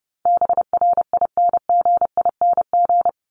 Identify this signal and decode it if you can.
Morse code, heard as a keyed tone, about 30 wpm, 700 Hz. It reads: BRINGING
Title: ING Suffix - Top 100 Words with Base Words 30wpm